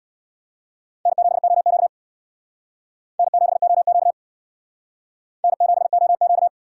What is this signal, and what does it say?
Morse code, heard as a keyed tone, about 68 wpm, 700 Hz. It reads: N7OQ N7OQ N7OQ